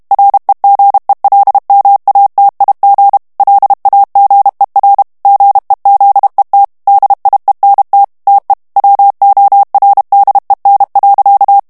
Morse code, heard as a keyed tone, about 32 wpm, 800 Hz. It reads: REGELMATIG LAGER GEZET DIENT TE WORDEN.
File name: HOLLANDAIS texte clair 2 vitesse 30 mots minute